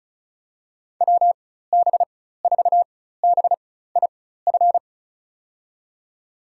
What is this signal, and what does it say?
Morse code, heard as a keyed tone, about 36 wpm, spaced as for 16 wpm, 690 Hz. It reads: WB4BIF
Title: USA Call Signs - 6 Characters - 4x Character Spacing - 35wpm